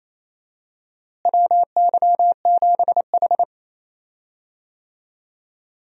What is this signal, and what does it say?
Morse code, heard as a keyed tone, about 27 wpm, 700 Hz. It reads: WY7H
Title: USA Call Signs - 4 Characters - Repeated 3x - 28wpm